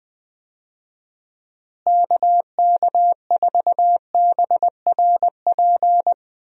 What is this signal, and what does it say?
Morse code, heard as a keyed tone, about 20 wpm, 700 Hz. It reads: KK4BRP
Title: USA Call Signs - 6 Characters - Repeated 2x - 20wpm